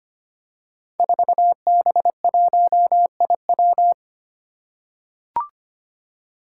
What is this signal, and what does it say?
Morse code, heard as a keyed tone, about 25 wpm, 700 Hz. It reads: 4B1IW
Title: Five Challenging Letter-Number Combinations 25wpm